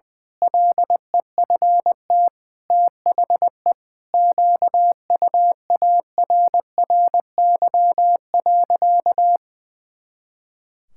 Morse code, heard as a keyed tone, about 21 wpm, 700 Hz. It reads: LEFT THE QUARRY.
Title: Pangram Challenge v2 - 20wpm